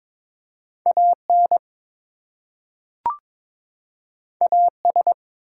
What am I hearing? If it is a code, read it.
Morse code, heard as a keyed tone, about 22 wpm, 700 Hz. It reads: AN AS
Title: Two Letter Words - 22wpm